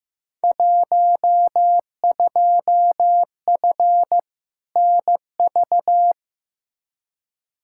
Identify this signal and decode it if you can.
Morse code, heard as a keyed tone, about 15 wpm, 700 Hz. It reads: 12F NV
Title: Field Day Exchanges Encore - 15wpm